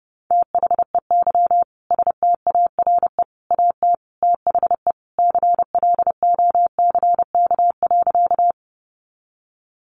Morse code, heard as a keyed tone, about 30 wpm, 700 Hz. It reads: THEY STARE AT THE CLOCK.
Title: Easy Phrases 30wpm